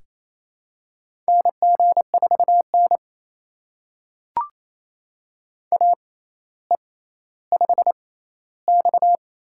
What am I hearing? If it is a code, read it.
Morse code, heard as a keyed tone, about 28 wpm, spaced as for 14 wpm, 700 Hz. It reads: NG4N AE5X
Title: USA Call Signs - 4 Characters - 6x Character Spacing - 28wpm